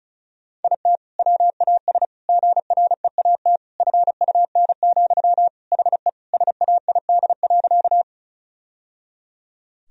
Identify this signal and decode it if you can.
Morse code, heard as a keyed tone, about 36 wpm, 700 Hz. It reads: IT WAS GREAT FUN, HE SAID.
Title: Sentences from Top 1000 Words - Review Part 4 - 35wpm